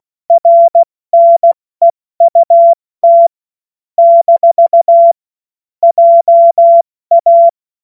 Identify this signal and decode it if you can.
Morse code, heard as a keyed tone, about 16 wpm, 680 Hz. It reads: RNEUT - JA
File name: Morsekurs / Witze / Witz 21. September 2021, 70 F80